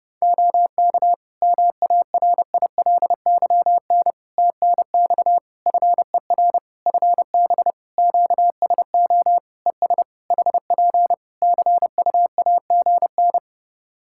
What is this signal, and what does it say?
Morse code, heard as a keyed tone, about 30 wpm, 700 Hz. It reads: OK MARILYN TNX FER FB QSO ES HP CUAGN
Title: Practice QSO - Exchange 4 - Ending - 30wpm